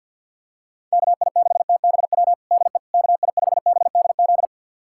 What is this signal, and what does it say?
Morse code, heard as a keyed tone, about 47 wpm, 700 Hz. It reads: KI6TBW DE KI5BDZ
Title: USA Callsign DE Callsign - 6 Characters - 50wpm